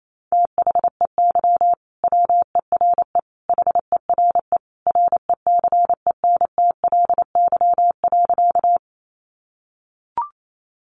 Morse code, heard as a keyed tone, about 28 wpm, 700 Hz. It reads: THEY WERE HERE RECENTLY.
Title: Sentences from Top 1200 Words 28wpm